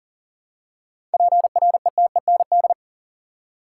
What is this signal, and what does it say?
Morse code, heard as a keyed tone, about 40 wpm, 700 Hz. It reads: PRETEND
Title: Seven Letter Words - 40wpm